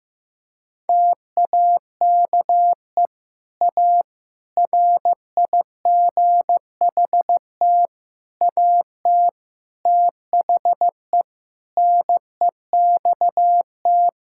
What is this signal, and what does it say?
Morse code, heard as a keyed tone, about 15 wpm, 700 Hz. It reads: TAKE A RIGHT AT THE NEXT
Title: Sentences from Top 1600 Words 15wpm